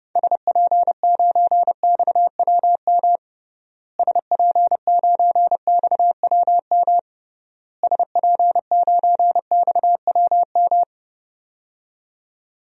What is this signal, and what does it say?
Morse code, heard as a keyed tone, about 32 wpm, 700 Hz. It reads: SP9XWM SP9XWM SP9XWM
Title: DXCC-100 Call Signs - 6 Characters - Repeated 3x - 30wpm